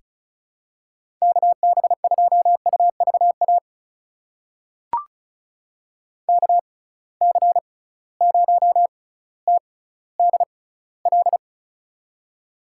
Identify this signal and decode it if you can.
Morse code, heard as a keyed tone, about 34 wpm, spaced as for 18 wpm, 700 Hz. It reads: KB2UVA KC0TDL
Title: USA Call Signs - 6 Characters - 6x Character Spacing - 35wpm